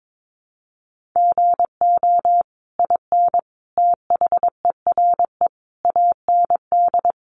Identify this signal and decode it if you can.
Morse code, heard as a keyed tone, about 22 wpm, 700 Hz. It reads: GO IN THERE AND
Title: Sentences from Top 500 Words 22wpm